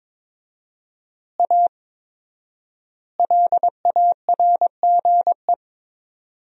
Morse code, heard as a keyed tone, about 22 wpm, 700 Hz. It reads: A LARGE
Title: 4x Word-Spacing - Sentences from Top 500 Words - 22wpm